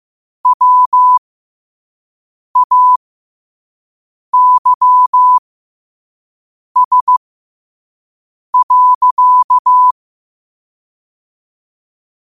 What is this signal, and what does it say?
Morse code, heard as a keyed tone, about 15 wpm, spaced as for 5 wpm, 1000 Hz. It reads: WAYS.
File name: qotdfiles / 5wpm